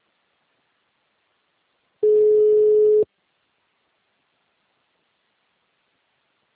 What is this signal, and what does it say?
Ringback tone (Europe, 425 Hz).
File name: out- -2034- - - .26